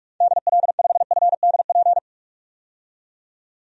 Morse code, heard as a keyed tone, about 43 wpm, 700 Hz. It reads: DL5FDP
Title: DXCC-100 Call Signs - 6 Characters - 45wpm